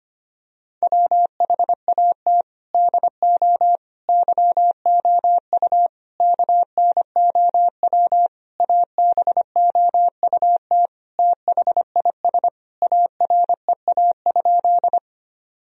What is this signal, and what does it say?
Morse code, heard as a keyed tone, about 25 wpm, 700 Hz. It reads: WHAT DO YOU KNOW ABOUT THIS AREA?
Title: Sentences from Top 1000 Words - Review Part 2 - 25wpm